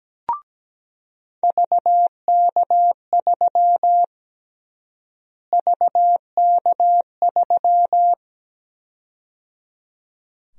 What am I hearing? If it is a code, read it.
Morse code, heard as a keyed tone, about 17 wpm, 700 Hz. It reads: VK3 VK3
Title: USA Call Sign Prefixes - Repeated 2x - 17wpm